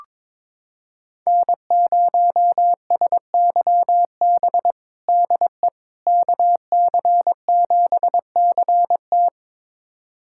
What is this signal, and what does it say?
Morse code, heard as a keyed tone, about 22 wpm, 700 Hz. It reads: N0SYB DE KC7CT
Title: USA Callsign DE Callsign - 5 Characters - 22wpm